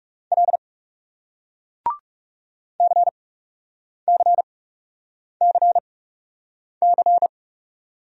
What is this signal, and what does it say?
Morse code, heard as a keyed tone, about 40 wpm, spaced as for 8 wpm, 700 Hz. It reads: L CCCC